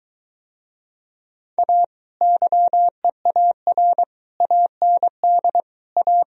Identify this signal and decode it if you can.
Morse code, heard as a keyed tone, about 23 wpm, 700 Hz. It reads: A YEAR AND A